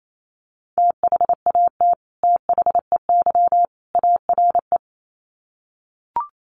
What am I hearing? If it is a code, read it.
Morse code, heard as a keyed tone, about 28 wpm, 700 Hz. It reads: THAT THEY ARE
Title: Sets of 3 Words 28wpm